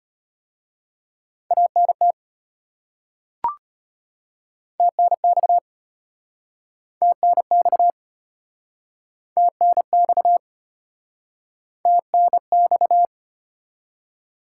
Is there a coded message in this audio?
Yes: Morse code from 1.5 to 13.1 s, heard as a keyed tone, about 32 wpm, 700 Hz. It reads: ANT TNX TNX TNX TNX